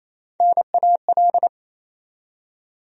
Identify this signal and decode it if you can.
Morse code, heard as a keyed tone, about 28 wpm, 700 Hz. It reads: NAL
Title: USA Call Sign Postfixes - 3 Letters 28wpm